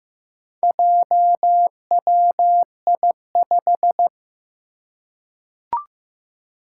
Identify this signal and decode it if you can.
Morse code, heard as a keyed tone, about 15 wpm, 700 Hz. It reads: JWI5
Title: Four Challenging Letter-Number Combinations 15wpm